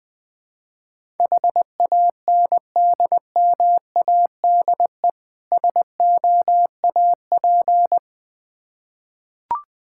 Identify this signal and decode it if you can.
Morse code, heard as a keyed tone, about 20 wpm, 700 Hz. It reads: HANDMADE SOAP
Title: Difficult Words with Hints - 20wpm